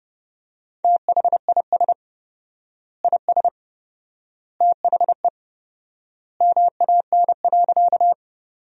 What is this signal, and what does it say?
Morse code, heard as a keyed tone, about 30 wpm, 700 Hz. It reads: THIS IS THE MAN.
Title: 4x Word-Spacing - Sentences from Top 200 Words - 30wpm